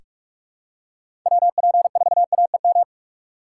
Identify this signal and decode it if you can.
Morse code, heard as a keyed tone, about 46 wpm, 700 Hz. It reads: WP4AEM
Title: USA Call Signs - 6 Characters - 4x Character Spacing - 45wpm